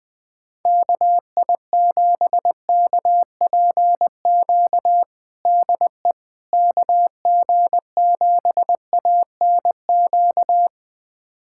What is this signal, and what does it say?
Morse code, heard as a keyed tone, about 20 wpm, 700 Hz. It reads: KI7KPQ DE KG7ANQ
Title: USA Callsign DE Callsign - 6 Characters - 20wpm